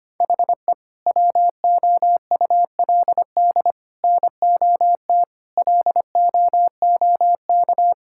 Morse code, heard as a keyed tone, about 25 wpm, 700 Hz. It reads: HE WOULD NOT LOOK
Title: Sentences from Top 1000 Words - Review Part 6 - 25wpm